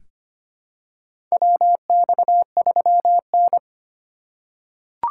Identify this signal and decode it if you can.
Morse code, heard as a keyed tone, about 25 wpm, 700 Hz. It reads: WX3N
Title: USA Call Signs - 2x Character Spacing - 25wpm